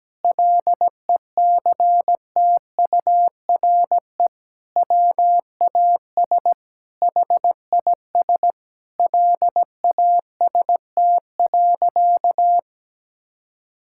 Morse code, heard as a keyed tone, about 17 wpm, 700 Hz. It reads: LECTURE WAS HIS LAST.
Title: Sentences from Top 1600 Words 17wpm